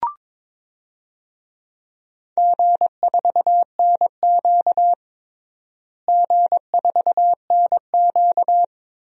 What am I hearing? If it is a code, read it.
Morse code, heard as a keyed tone, about 22 wpm, 700 Hz. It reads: G4NQ G4NQ
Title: DXCC-100 Call Signs - 4 Characters - Repeated 2x - 22wpm